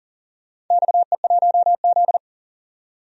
Morse code, heard as a keyed tone, about 40 wpm, 700 Hz. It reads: XE1Z
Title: DXCC-100 Call Signs - 4 Characters - 40wpm